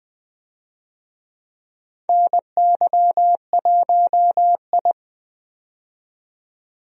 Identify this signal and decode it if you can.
Morse code, heard as a keyed tone, about 20 wpm, 700 Hz. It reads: NY1I
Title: USA Call Signs - 4 Characters - 20wpm